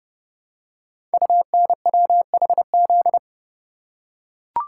Morse code, heard as a keyed tone, about 30 wpm, 700 Hz. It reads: UNWHZ
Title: Code Groups - 5 Letters - 30wpm